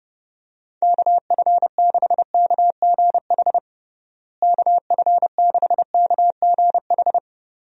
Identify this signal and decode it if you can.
Morse code, heard as a keyed tone, about 30 wpm, 700 Hz. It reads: KF6KGH KF6KGH
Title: USA Call Signs - 6 Characters - Repeated 3x - 30wpm